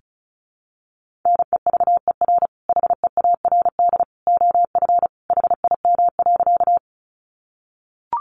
Morse code, heard as a keyed tone, about 34 wpm, 700 Hz. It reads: NEVER HEARD OF HIM.
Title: Sentences from Top 400 Words 35wpm